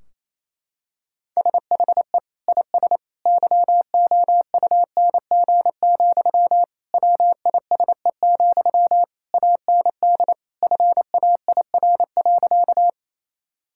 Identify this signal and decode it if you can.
Morse code, heard as a keyed tone, about 28 wpm, 700 Hz. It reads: SHE IS YOUNG, WISE, AND FAIR.
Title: Sentences from Top 1600 Words 28wpm